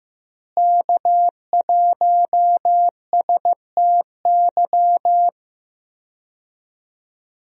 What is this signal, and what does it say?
Morse code, heard as a keyed tone, about 15 wpm, 700 Hz. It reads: K1STY
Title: USA Call Signs - 5 Characters - Repeated 3x - 15wpm